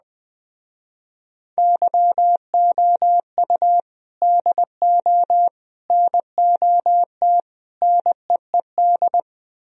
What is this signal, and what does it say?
Morse code, heard as a keyed tone, about 20 wpm, 700 Hz. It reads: YOU DO NOT NEED
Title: Speed-Racing - Sets of 4 Words 17wpm